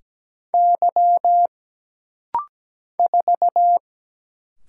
Morse code, heard as a keyed tone, about 17 wpm, 700 Hz. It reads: Y 4
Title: Single Letter-Number Alt - Rapid-Fire 17wpm